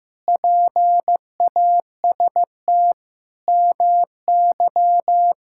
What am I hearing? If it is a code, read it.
Morse code, heard as a keyed tone, about 15 wpm, 700 Hz. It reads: PAST MY
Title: Pangram Challenge v2 - 15wpm